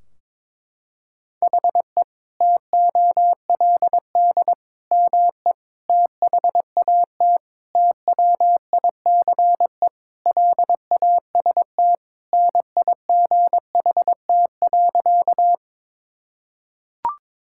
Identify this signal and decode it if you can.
Morse code, heard as a keyed tone, about 22 wpm, 700 Hz. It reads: HE TOLD ME THAT TWICE LAST NIGHT.
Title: Sentences from Top 1600 Words 22wpm